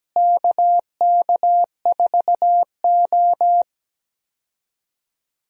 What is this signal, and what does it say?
Morse code, heard as a keyed tone, about 17 wpm, 700 Hz. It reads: KK4O
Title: USA Call Signs - 4 Characters - Repeated 2x - 17wpm